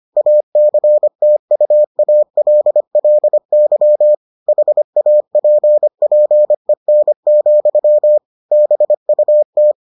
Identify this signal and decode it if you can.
Morse code, heard as a keyed tone, about 25 wpm, 600 Hz. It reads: ACTUALLY HAPPEN, BUT